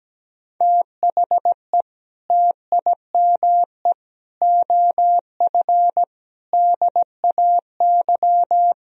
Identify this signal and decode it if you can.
Morse code, heard as a keyed tone, about 17 wpm, 700 Hz. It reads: THE TIME OF DAY
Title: Sentences from Top 1000 Words - Review Part 3 - 17wpm